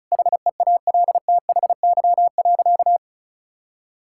Morse code, heard as a keyed tone, about 36 wpm, 700 Hz. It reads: HEALTHY.
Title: Sentences from Top 1000 Words - Review Part 5 - 35wpm